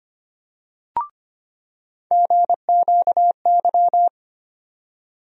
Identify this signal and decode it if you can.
Morse code, heard as a keyed tone, about 25 wpm, 700 Hz. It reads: GQY
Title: USA Call Sign Postfixes - 3 Letters 25wpm